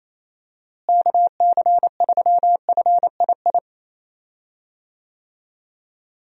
Morse code, heard as a keyed tone, about 28 wpm, 700 Hz. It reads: KC3FII
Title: USA Call Signs - 6 Characters - Repeated 2x - 28wpm